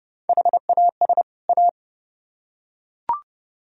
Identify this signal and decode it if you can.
Morse code, heard as a keyed tone, about 28 wpm, 700 Hz. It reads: HAS A